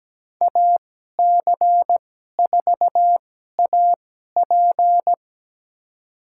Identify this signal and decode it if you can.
Morse code, heard as a keyed tone, about 17 wpm, spaced as for 12 wpm, 700 Hz. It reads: AC4AP
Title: USA Call Signs - 2x Character Spacing - 17wpm